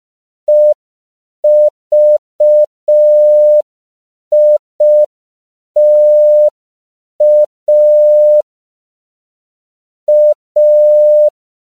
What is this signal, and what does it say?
Morse code, heard as a keyed tone, about 5 wpm, 600 Hz. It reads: EVITA A
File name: CW / 5PPM